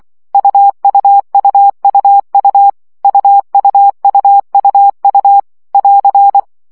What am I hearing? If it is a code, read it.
Morse code, heard as a keyed tone, about 24 wpm, 800 Hz. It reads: UUUUU UUUUU <AR>
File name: CW gyakorlas u 20 baud